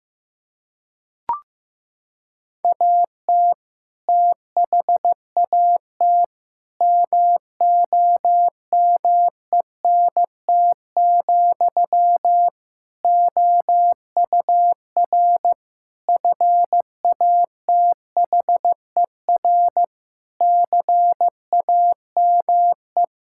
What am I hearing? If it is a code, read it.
Morse code, heard as a keyed tone, about 16 wpm, 700 Hz. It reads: AT THAT MOMENT, OUR FATHER CAME
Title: Sentences from Top 1600 Words 15wpm